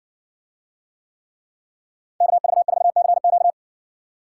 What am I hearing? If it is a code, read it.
Morse code, heard as a keyed tone, about 60 wpm, 700 Hz. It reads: KV4CQ